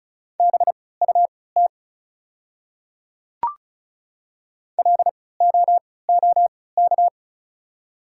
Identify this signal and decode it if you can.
Morse code, heard as a keyed tone, about 36 wpm, spaced as for 20 wpm, 700 Hz. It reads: BUT LOOK
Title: 3x Character Spacing - Top 200 Words - 35wpm